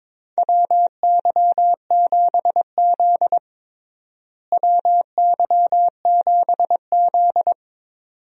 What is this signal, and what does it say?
Morse code, heard as a keyed tone, about 21 wpm, 700 Hz. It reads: WY7Z WY7Z
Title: USA Call Signs - 4 Characters - Repeated 2x - 22wpm